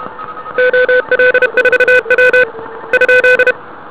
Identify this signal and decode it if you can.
Morse code, heard as a keyed tone, about 32 wpm, 1500 Hz, with noise. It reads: OL4W ?
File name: ARCHIV / 80m cw